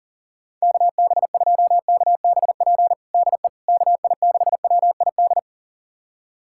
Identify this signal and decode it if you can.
Morse code, heard as a keyed tone, about 40 wpm, 700 Hz. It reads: KB2KBP DE KI6WID
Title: USA Callsign DE Callsign - 6 Characters - 40wpm